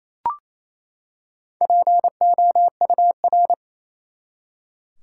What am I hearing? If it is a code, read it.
Morse code, heard as a keyed tone, about 28 wpm, 700 Hz. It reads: POUR
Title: Four Letter Words - 28wpm